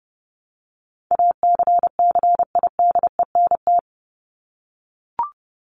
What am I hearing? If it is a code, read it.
Morse code, heard as a keyed tone, about 30 wpm, 700 Hz. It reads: ACCIDENT
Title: Top 1751-2000 Words 30wpm